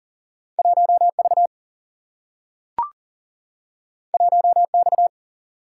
Morse code, heard as a keyed tone, about 39 wpm, 700 Hz. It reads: 1V 1X
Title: Single-Letter Single-Number 40wpm